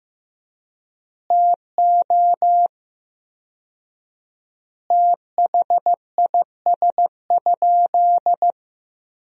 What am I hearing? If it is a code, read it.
Morse code, heard as a keyed tone, about 15 wpm, 700 Hz. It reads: TO THIS?
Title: 4x Word-Spacing - Sentences from Top 100 Words - 15wpm